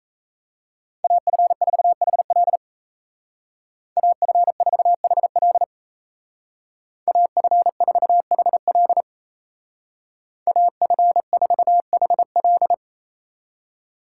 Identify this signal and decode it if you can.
Morse code, heard as a keyed tone, about 35 wpm, 700 Hz. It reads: AF4HL AF4HL AF4HL AF4HL
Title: Speed Racing - USA 2x2 Call Signs 28wpm